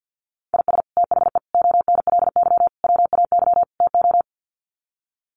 Morse code, heard as a keyed tone, about 50 wpm, 700 Hz. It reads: IS THE ONLY WAY TO
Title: Sets of 5 Words Encore 50wpm